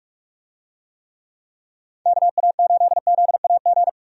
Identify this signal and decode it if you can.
Morse code, heard as a keyed tone, about 41 wpm, 700 Hz. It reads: KA8ZAG
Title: USA Call Signs - 6 Characters - Repeated 2x - 45wpm